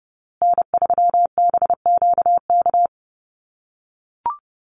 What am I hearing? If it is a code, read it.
Morse code, heard as a keyed tone, about 30 wpm, 700 Hz. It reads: N3BQK
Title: USA Call Signs Repeated 2x 30wpm